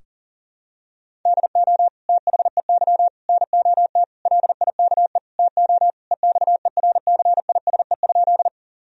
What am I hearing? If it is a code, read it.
Morse code, heard as a keyed tone, about 37 wpm, 700 Hz. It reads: DO THEY NOT LIKE TO EXERCISE?
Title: Sentences from Top 1000 Words - Review Part 4 - 40wpm